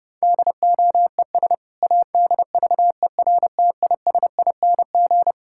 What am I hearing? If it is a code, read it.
Morse code, heard as a keyed tone, about 32 wpm, 700 Hz. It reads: DOES ADVERTISING
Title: Sentences from Top 1000 Words - Review Part 6 - 30wpm